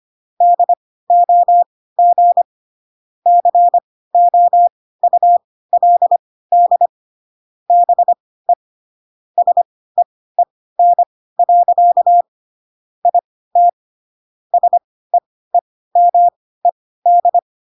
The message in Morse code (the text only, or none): DOG COULD BE SEEN. IT SEEMED